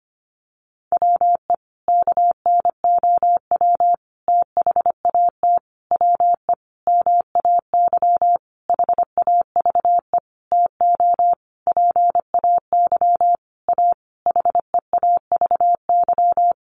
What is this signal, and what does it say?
Morse code, heard as a keyed tone, about 25 wpm, 700 Hz. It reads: WE KNOW THAT WE MAY HAVE TO PAY A HEAVY
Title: Sentences from Top 1100 Words 25wpm